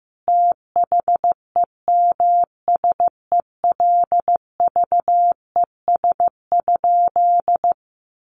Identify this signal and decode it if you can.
Morse code, heard as a keyed tone, about 15 wpm, 700 Hz. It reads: THEMSELVES?
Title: Sentences from Top 800 Words 15wpm